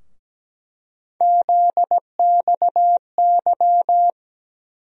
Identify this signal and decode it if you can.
Morse code, heard as a keyed tone, about 17 wpm, 700 Hz. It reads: ZXY
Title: USA Call Sign Postfixes - 3 Letters 17wpm